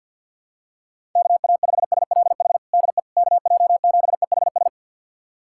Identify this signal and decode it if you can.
Morse code, heard as a keyed tone, about 51 wpm, 700 Hz. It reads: KA5SLH DE KJ7EHS
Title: USA Callsign DE Callsign 50wpm